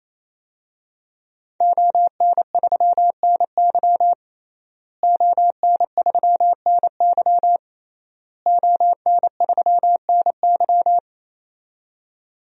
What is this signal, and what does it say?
Morse code, heard as a keyed tone, about 28 wpm, 700 Hz. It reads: ON3NY ON3NY ON3NY
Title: DXCC-100 Call Signs - 5 Characters - Repeated 3x - 28wpm